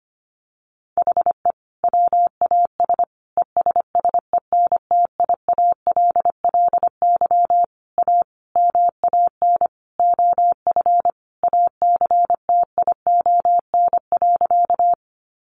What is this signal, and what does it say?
Morse code, heard as a keyed tone, about 25 wpm, 700 Hz. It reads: HE WAS ESSENTIALLY A MAN OF ACTION.
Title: Sentences from Top 1300 Words 25wpm